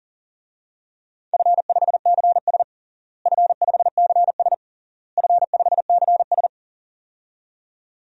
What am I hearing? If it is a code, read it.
Morse code, heard as a keyed tone, about 37 wpm, 700 Hz. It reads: F5CS F5CS F5CS
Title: DXCC-100 Call Signs - 4 Characters - Repeated 3x - 40wpm